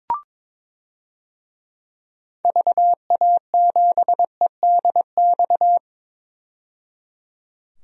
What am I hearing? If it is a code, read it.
Morse code, heard as a keyed tone, about 22 wpm, 700 Hz. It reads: VA7EDX
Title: DXCC-100 Call Signs - 6 Characters - 22wpm